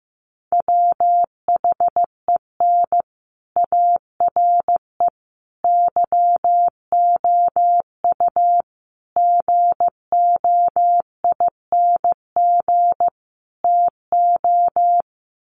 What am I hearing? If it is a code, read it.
Morse code, heard as a keyed tone, about 15 wpm, 700 Hz. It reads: WHEN ARE YOU GOING TO
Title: Sets of 5 Words Encore2 15wpm